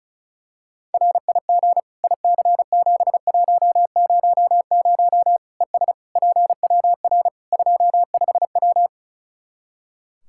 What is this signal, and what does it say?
Morse code, heard as a keyed tone, about 36 wpm, 700 Hz. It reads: RIG IC7100 ES PWR 25W
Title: K1USN SST QSO Practice 35wpm